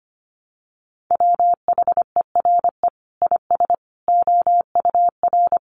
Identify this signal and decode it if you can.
Morse code, heard as a keyed tone, about 25 wpm, 700 Hz. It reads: WHERE IS OUR
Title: Sentences from Top 1300 Words 25wpm